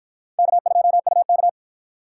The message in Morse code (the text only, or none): K2UK